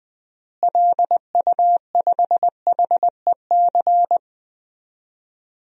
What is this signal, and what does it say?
Morse code, heard as a keyed tone, about 20 wpm, 700 Hz. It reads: LU5HEC
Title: DXCC-100 Call Signs - 6 Characters - Repeated 2x - 20wpm